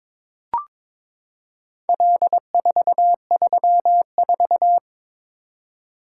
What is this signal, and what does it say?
Morse code, heard as a keyed tone, about 22 wpm, 700 Hz. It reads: L434
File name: practice / 22wpm / Four Letter-Number Combinations - 22wpm